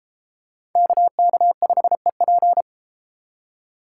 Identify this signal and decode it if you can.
Morse code, heard as a keyed tone, about 32 wpm, 700 Hz. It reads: KK5EP